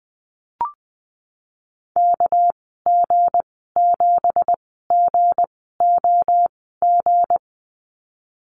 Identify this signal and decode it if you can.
Morse code, heard as a keyed tone, about 19 wpm, spaced as for 14 wpm, 700 Hz. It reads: KG7GOG
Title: 207 Lesson 33g - Letter j - Call Signs 20wpm